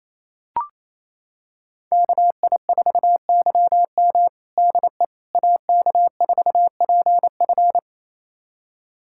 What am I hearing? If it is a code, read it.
Morse code, heard as a keyed tone, about 28 wpm, 690 Hz. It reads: KI4YM DE AK4PF
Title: USA Callsign DE Callsign - 5 Characters - 28wpm